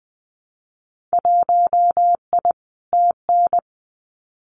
Morse code, heard as a keyed tone, about 20 wpm, 700 Hz. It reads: 1I TN
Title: Winter Field Day Exchanges Encore 20wpm